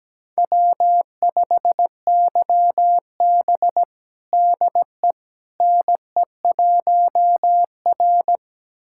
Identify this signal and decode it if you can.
Morse code, heard as a keyed tone, about 17 wpm, 700 Hz. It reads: W5YB DE NE1R
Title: USA Callsign DE Callsign - 4 Characters - 17wpm